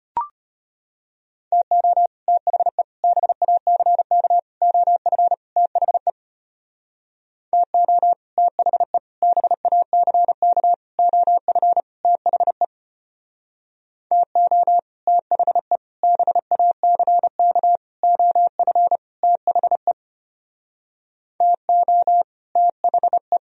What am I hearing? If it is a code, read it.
Morse code, heard as a keyed tone, about 32 wpm, 700 Hz. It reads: TO THE BACK OF THE TO THE BACK OF THE TO THE BACK OF THE TO THE